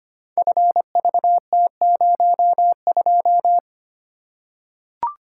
Morse code, heard as a keyed tone, about 25 wpm, 700 Hz. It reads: FVT02
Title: Code Groups - 5 Letter:Numbers - 25wpm